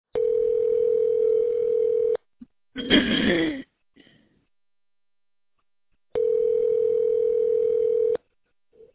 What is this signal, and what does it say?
Ringback tone (North America), 440 + 480 Hz.